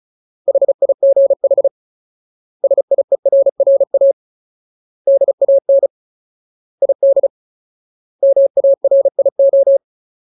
Morse code, heard as a keyed tone, about 34 wpm, 550 Hz. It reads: HIGH SIERRA DAN ID MARIO
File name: Ultimate Word List / 35WPM Ultimate